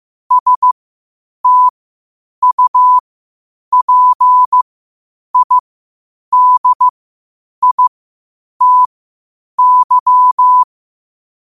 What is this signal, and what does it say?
Morse code, heard as a keyed tone, about 15 wpm, spaced as for 8 wpm, 1000 Hz. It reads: STUPIDITY
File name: qotdfiles / 8wpm